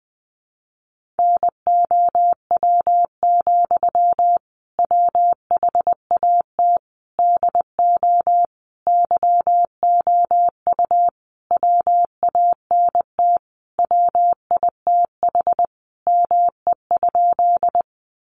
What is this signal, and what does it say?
Morse code, heard as a keyed tone, about 20 wpm, 700 Hz. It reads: NOW, WHAT DO YOU WANT WITH ME?
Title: Sentences from Top 200 Words 20wpm